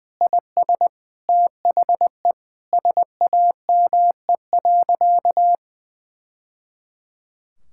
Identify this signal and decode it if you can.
Morse code, heard as a keyed tone, about 20 wpm, 700 Hz. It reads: IS THE SAME.
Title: Sentences from Top 1000 Words - Review Part 6 - 20wpm